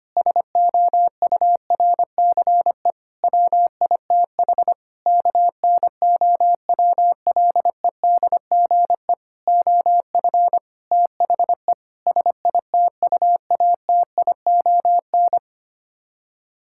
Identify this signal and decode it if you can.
Morse code, heard as a keyed tone, about 25 wpm, 700 Hz. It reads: SOURCE WITH KNOWLEDGE OF THE SITUATION
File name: Sets of 9 Words - 25wpm